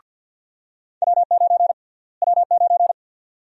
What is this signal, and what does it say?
Morse code, heard as a keyed tone, about 51 wpm, 700 Hz. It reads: W9 W9
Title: USA Call Sign Prefixes - Repeated 2x - 50wpm